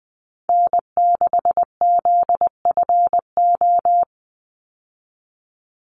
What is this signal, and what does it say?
Morse code, heard as a keyed tone, about 20 wpm, 700 Hz. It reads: N6ZFO
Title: Call Signs Repeated 2x - 20wpm